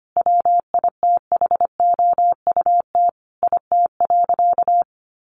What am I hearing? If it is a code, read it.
Morse code, heard as a keyed tone, about 25 wpm, 700 Hz. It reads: WITHOUT IT.
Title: Sentences from Top 600 Words 25wpm